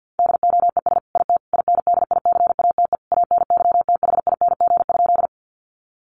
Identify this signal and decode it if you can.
Morse code, heard as a keyed tone, about 49 wpm, 700 Hz. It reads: DOES IT INDICATE ANYTHING?
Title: Sentences from Top 1300 Words 50wpm